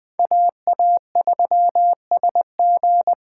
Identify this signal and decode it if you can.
Morse code, heard as a keyed tone, about 21 wpm, 700 Hz. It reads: AA3SG